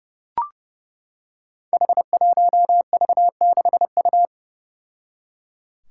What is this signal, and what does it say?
Morse code, heard as a keyed tone, about 32 wpm, 700 Hz. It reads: H1V6U
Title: Five Challenging Letter-Number Combinations 30wpm